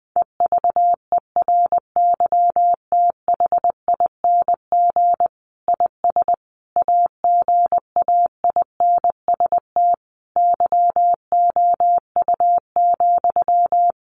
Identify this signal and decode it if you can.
Morse code, heard as a keyed tone, about 20 wpm, 700 Hz. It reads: EVERYTHING IS AGAINST YOU,
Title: Sentences from Top 700 Words 20wpm